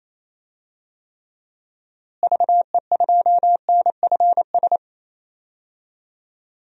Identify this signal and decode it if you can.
Morse code, heard as a keyed tone, about 28 wpm, 700 Hz. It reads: VE2NFS
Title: DXCC-100 Call Signs - 6 Characters - Repeated 2x - 28wpm